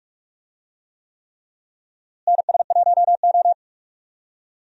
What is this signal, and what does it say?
Morse code, heard as a keyed tone, about 45 wpm, 700 Hz. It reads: NS1O
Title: USA Call Signs - 4 Characters - Repeated 3x - 45wpm